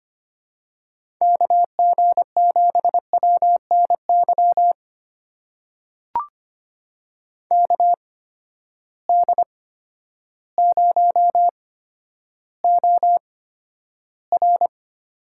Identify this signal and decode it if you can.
Morse code, heard as a keyed tone, about 25 wpm, spaced as for 11 wpm, 700 Hz. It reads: KG7WNY KD0OR
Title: USA Call Signs - 6 Characters - 8x Character Spacing - 25wpm